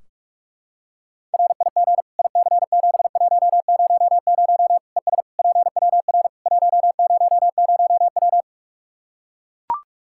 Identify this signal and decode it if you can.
Morse code, heard as a keyed tone, about 42 wpm, 700 Hz. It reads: RIG IC7100 ES PWR 100W
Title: Practice QSO - Exchange 2 - Info - 45wpm